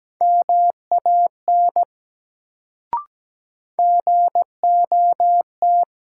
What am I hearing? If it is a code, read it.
Morse code, heard as a keyed tone, about 17 wpm, 700 Hz. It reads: MAN GOT
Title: Sets of 2 Words - New Word Drill - 17wpm